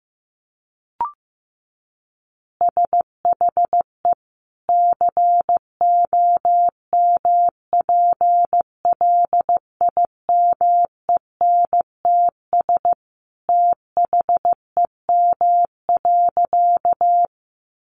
Morse code, heard as a keyed tone, about 15 wpm, 700 Hz. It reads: SHE COMPLIMENTS THEM.
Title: Easy Phrases 15wpm